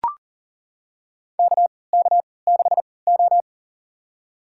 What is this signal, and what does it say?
Morse code, heard as a keyed tone, about 40 wpm, spaced as for 23 wpm, 700 Hz. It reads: KK6O
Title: USA Call Signs - 3x Character Spacing - 40wpm